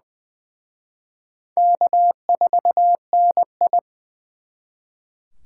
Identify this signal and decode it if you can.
Morse code, heard as a keyed tone, about 24 wpm, 700 Hz. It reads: K4NI
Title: Speed Racing - USA 1x2 Call Signs 20wpm